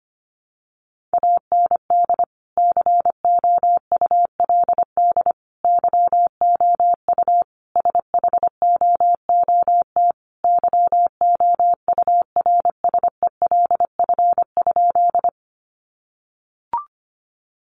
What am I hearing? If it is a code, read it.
Morse code, heard as a keyed tone, about 25 wpm, 700 Hz. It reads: AND COULD YOU SHOOT YOURSELF?
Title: Sentences from Top 1200 Words 25wpm